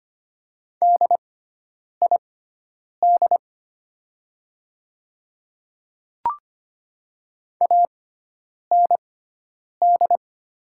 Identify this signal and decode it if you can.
Morse code, heard as a keyed tone, about 25 wpm, spaced as for 9 wpm, 700 Hz. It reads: DID AND
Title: 6x Character Spacing - Top 100 Words - 25wpm